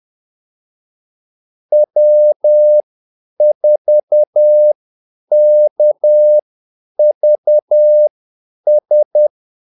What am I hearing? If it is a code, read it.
Morse code, heard as a keyed tone, about 10 wpm, spaced as for 8 wpm, 600 Hz. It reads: W4KVS
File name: Callsigns-10-8wpm